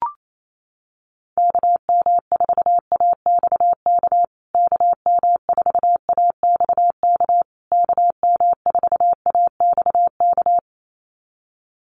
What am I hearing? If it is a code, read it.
Morse code, heard as a keyed tone, about 28 wpm, 700 Hz. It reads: KM4AXK KM4AXK KM4AXK
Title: USA Call Signs Repeated 3x 28wpm